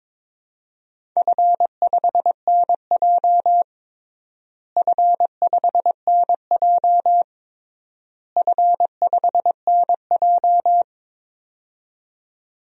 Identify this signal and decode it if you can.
Morse code, heard as a keyed tone, about 22 wpm, 700 Hz. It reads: F5NJ F5NJ F5NJ
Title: DXCC-100 Call Signs - 4 Characters - Repeated 3x - 22wpm